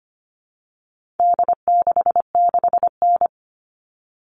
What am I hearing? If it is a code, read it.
Morse code, heard as a keyed tone, about 25 wpm, 700 Hz. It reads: D66N
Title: All Cut Numbers - 4-Digit 25wpm